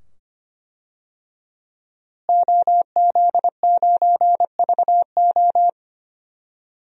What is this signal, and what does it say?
Morse code, heard as a keyed tone, about 25 wpm, 700 Hz. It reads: OZ9VO
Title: DXCC-100 Call Signs - 5 Characters - Repeated 2x - 25wpm